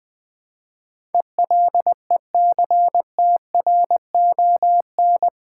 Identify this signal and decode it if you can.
Morse code, heard as a keyed tone, about 20 wpm, 700 Hz. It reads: ELECTRON
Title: Build A Word - 4 Letters At A Time - 10-Letter Words - 20wpm